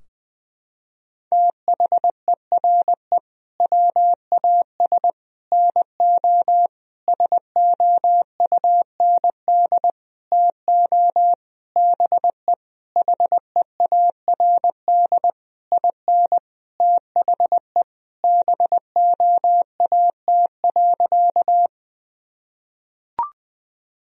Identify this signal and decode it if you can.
Morse code, heard as a keyed tone, about 21 wpm, 700 Hz. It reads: THERE WAS NO SOUND TO BE HEARD IN THE BOAT.
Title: Sentences from Top 1000 Words - Review Part 3 - 20wpm